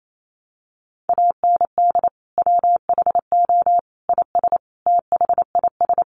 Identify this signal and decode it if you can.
Morse code, heard as a keyed tone, about 28 wpm, 700 Hz. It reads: AND WHO IS THIS
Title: Sentences from Top 1100 Words 28wpm